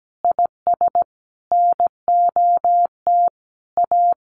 Morse code, heard as a keyed tone, about 17 wpm, 700 Hz. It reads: IS NOT A
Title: Sentences from Top 1400 Words 17wpm